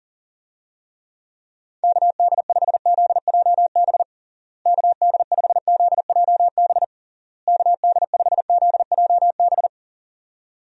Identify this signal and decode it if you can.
Morse code, heard as a keyed tone, about 40 wpm, 700 Hz. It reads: KD5ZJB KD5ZJB KD5ZJB
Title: USA Call Signs - 6 Characters - Repeated 3x - 40wpm